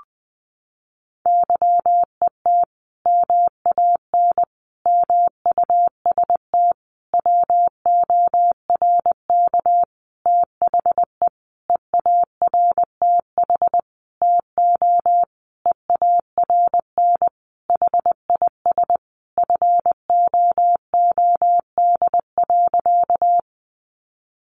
Morse code, heard as a keyed tone, about 20 wpm, 700 Hz. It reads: YET MAN MUST WORK THE EARTH TO EARN HIS FOOD.
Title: Sentences from Top 1500 Words 20wpm